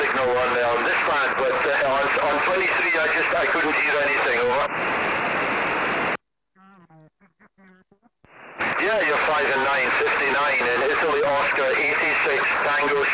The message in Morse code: none

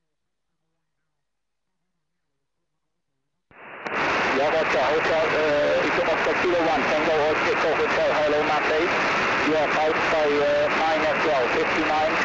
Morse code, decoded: none

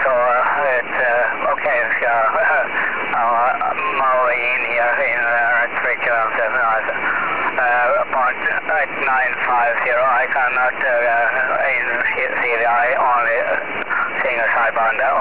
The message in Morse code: none

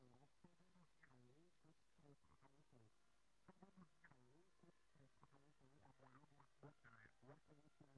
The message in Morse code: none

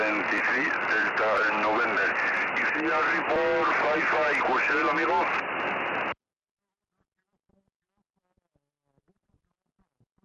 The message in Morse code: none

